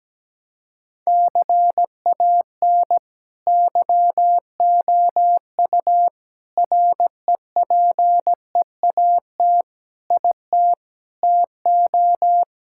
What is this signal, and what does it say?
Morse code, heard as a keyed tone, about 17 wpm, 700 Hz. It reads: CAN YOU REPEAT IT TO
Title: Sentences from Top 1600 Words 17wpm